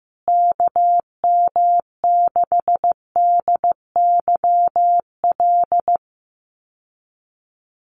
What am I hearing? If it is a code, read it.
Morse code, heard as a keyed tone, about 15 wpm, 700 Hz. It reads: KM6DYL
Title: USA Call Signs Repeated 3x 15wpm